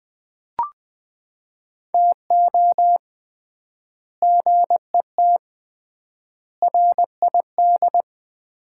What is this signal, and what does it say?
Morse code, heard as a keyed tone, about 20 wpm, 700 Hz. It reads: TO GET RID
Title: 3x Word-Spacing - Sets of 5 Words - 20wpm